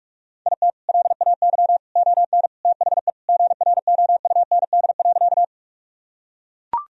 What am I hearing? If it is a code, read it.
Morse code, heard as a keyed tone, about 41 wpm, 700 Hz. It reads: IT LAY ON THE GROUND.
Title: Sentences from Top 1000 Words - Review Part 2 - 45wpm